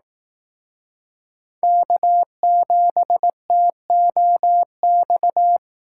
Morse code, heard as a keyed tone, about 18 wpm, 700 Hz. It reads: K7TOX
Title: Speed Racing - USA 1x3 Call Signs 15wpm